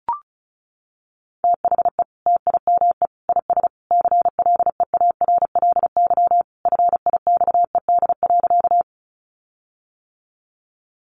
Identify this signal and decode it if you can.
Morse code, heard as a keyed tone, about 34 wpm, 700 Hz. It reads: THE TIME IS CLEARLY FIXED.
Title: Sentences from Top 1300 Words 35wpm